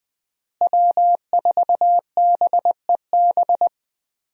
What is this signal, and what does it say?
Morse code, heard as a keyed tone, about 20 wpm, 700 Hz. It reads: W4BEB
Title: USA Call Signs - 5 Characters - Repeated 2x - 20wpm